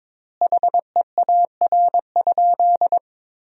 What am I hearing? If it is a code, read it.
Morse code, heard as a keyed tone, about 22 wpm, 700 Hz. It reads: HEAR?
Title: 3x Word-Spacing - Sentences from Top 300 Words - 22wpm